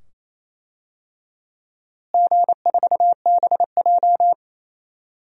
Morse code, heard as a keyed tone, about 28 wpm, 700 Hz. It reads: G4BJ
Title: DXCC-100 Call Signs - 4 Characters - Repeated 2x - 28wpm